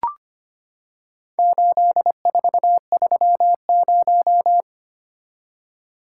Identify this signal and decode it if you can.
Morse code, heard as a keyed tone, about 25 wpm, 700 Hz. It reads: 8430
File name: practice / 25wpm / Four Letter-Number Combinations - 25wpm